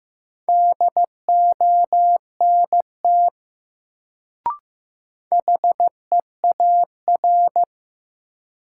Morse code, heard as a keyed tone, about 15 wpm, 700 Hz. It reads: DONT HEAR
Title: Sentences from Top 300 Words - New Word Drill - 15wpm